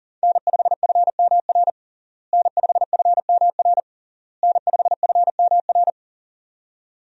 Morse code, heard as a keyed tone, about 37 wpm, 700 Hz. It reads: N5FMR N5FMR N5FMR
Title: USA Call Signs - 5 Characters - Repeated 3x - 40wpm